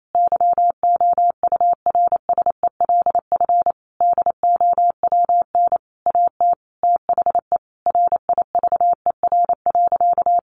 Morse code, heard as a keyed tone, about 28 wpm, 700 Hz. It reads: YOURSELF DOWN AT THE RIVER.
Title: Sentences from Top 1200 Words 28wpm